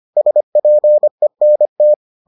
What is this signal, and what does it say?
Morse code, heard as a keyed tone, about 25 wpm, 600 Hz. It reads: SPENT